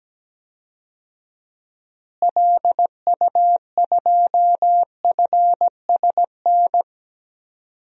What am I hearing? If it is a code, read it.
Morse code, heard as a keyed tone, about 17 wpm, 700 Hz. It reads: LU2FSN
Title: DXCC-100 Call Signs - 6 Characters - Repeated 3x - 17wpm